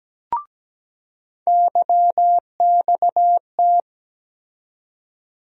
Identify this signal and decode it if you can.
Morse code, heard as a keyed tone, about 17 wpm, 700 Hz. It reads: YXT
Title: USA Call Sign Postfixes - 3 Letters 17wpm